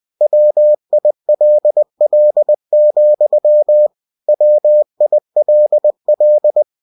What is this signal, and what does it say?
Morse code, heard as a keyed tone, about 20 wpm, 600 Hz. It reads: WILL, WILL
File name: cw-books / winnie-the-pooh / winnie-20wpm-page-0102